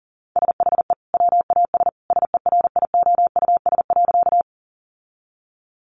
Morse code, heard as a keyed tone, about 37 wpm, 700 Hz. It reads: SHE WAS SERIOUS.
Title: Sentences from Top 1100 Words 40wpm